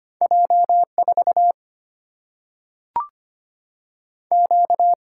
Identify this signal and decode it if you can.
Morse code, heard as a keyed tone, about 25 wpm, 700 Hz. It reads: J4 Q
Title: Single-Letter Single-Number 25wpm